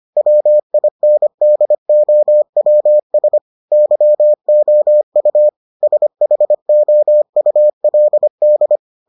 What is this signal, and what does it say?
Morse code, heard as a keyed tone, about 25 wpm, 600 Hz. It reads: WINDOWS YOU SHOULD